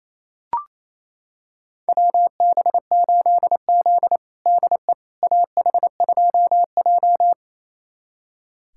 Morse code, heard as a keyed tone, about 28 wpm, 700 Hz. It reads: WB8Z DE AH2J
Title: USA Callsign DE Callsign - 4 Characters - 28wpm